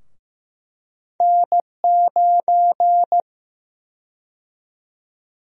Morse code, heard as a keyed tone, about 15 wpm, 700 Hz. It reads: N9
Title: USA Call Sign Prefixes - 15wpm